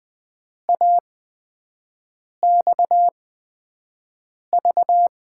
Morse code, heard as a keyed tone, about 20 wpm, spaced as for 5 wpm, 700 Hz. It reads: AXV
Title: USA Call Signs - 6 Characters - 8x Character Spacing - 20wpm